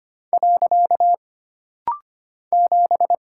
Morse code, heard as a keyed tone, about 25 wpm, 700 Hz. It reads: . 7
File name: Single Character (Letter:Number:Punctuation:Prosign) - Rapid-Fire - 25wpm